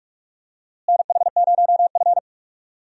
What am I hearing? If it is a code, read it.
Morse code, heard as a keyed tone, about 43 wpm, 700 Hz. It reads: NH0F
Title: USA Call Signs - 4 Characters - Repeated 2x - 45wpm